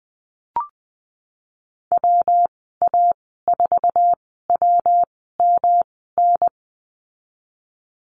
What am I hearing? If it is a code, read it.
Morse code, heard as a keyed tone, about 20 wpm, spaced as for 15 wpm, 700 Hz. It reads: WA4WMN
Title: 173 Lesson 28g - Number 9 - Call Signs 20wpm